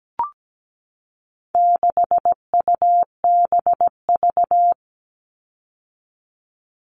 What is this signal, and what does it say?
Morse code, heard as a keyed tone, about 17 wpm, 700 Hz. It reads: 6UBV
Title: All Cut Numbers - 4-Digit 17wpm